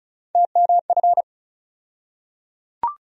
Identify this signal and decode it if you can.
Morse code, heard as a keyed tone, about 34 wpm, 700 Hz. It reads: TMF
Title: USA Call Sign Postfixes - 3 Letters 35wpm